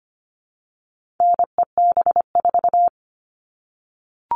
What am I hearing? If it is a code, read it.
Morse code, heard as a keyed tone, about 25 wpm, 700 Hz. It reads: NEB4
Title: All Cut Numbers - 4-Digit 25wpm